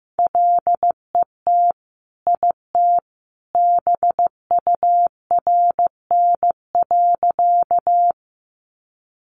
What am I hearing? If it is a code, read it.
Morse code, heard as a keyed tone, about 15 wpm, 700 Hz. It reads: LET IT BURN.
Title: Sentences from Top 1300 Words 15wpm